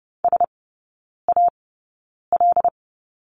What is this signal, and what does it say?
Morse code, heard as a keyed tone, about 30 wpm, spaced as for 9 wpm, 700 Hz. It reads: SAL
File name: course / 30:10wpm / 087 Lesson 15e - Number 5 - Three Characters 30wpm